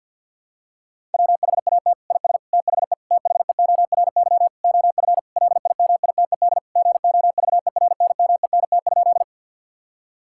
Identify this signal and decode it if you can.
Morse code, heard as a keyed tone, about 50 wpm, 700 Hz. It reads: WHAT IS THE THEORY OF LIMITED GOVERNMENT?
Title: Sentences from Top 1000 Words - Review Part 6 - 50wpm